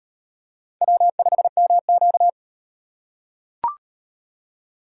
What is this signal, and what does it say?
Morse code, heard as a keyed tone, about 38 wpm, 700 Hz. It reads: W5MQ